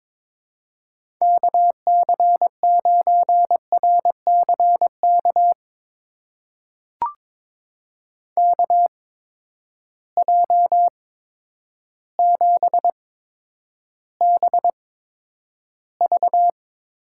Morse code, heard as a keyed tone, about 22 wpm, spaced as for 10 wpm, 700 Hz. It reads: KC9RCK KJ7BV
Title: USA Call Signs - 6 Characters - 8x Character Spacing - 22wpm